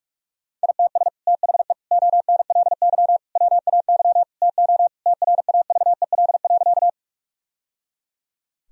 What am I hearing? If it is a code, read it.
Morse code, heard as a keyed tone, about 41 wpm, 700 Hz. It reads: ITS THE ONLY WAY TO TRAVEL.
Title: Sentences from Top 1000 Words - Review Part 2 - 45wpm